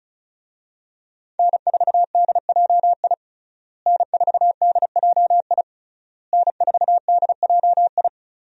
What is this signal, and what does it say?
Morse code, heard as a keyed tone, about 36 wpm, 700 Hz. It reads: N4DJI N4DJI N4DJI
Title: USA Call Signs - 5 Characters - Repeated 3x - 35wpm